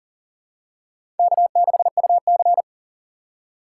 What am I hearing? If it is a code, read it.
Morse code, heard as a keyed tone, about 40 wpm, 700 Hz. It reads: K6UC
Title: USA Call Signs - 3x Character Spacing - 40wpm